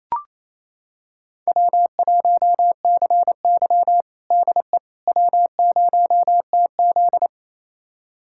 Morse code, heard as a keyed tone, about 28 wpm, 700 Hz. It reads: W1CY DE W0TZ
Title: USA Callsign DE Callsign - 4 Characters - 28wpm